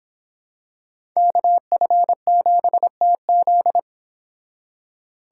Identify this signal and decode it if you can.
Morse code, heard as a keyed tone, about 26 wpm, 700 Hz. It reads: KF7TZ